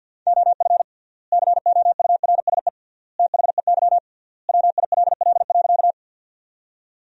Morse code, heard as a keyed tone, about 50 wpm, 700 Hz. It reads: OF COURSE THEY WILL.
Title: 3x Word-Spacing - Sentences from Top 500 Words - 50wpm